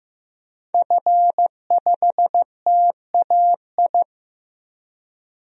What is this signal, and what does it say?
Morse code, heard as a keyed tone, about 15 wpm, 700 Hz. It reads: F5TAI
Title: DXCC-100 Call Signs - 5 Characters - 15wpm